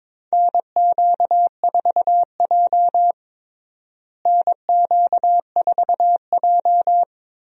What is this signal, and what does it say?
Morse code, heard as a keyed tone, about 21 wpm, 700 Hz. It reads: NQ4J NQ4J
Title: USA Call Signs - 4 Characters - Repeated 3x - 22wpm